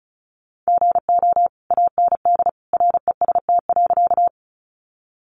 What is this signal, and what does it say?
Morse code, heard as a keyed tone, about 34 wpm, 700 Hz. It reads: GO AND REST.
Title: Sentences from Top 500 Words 35wpm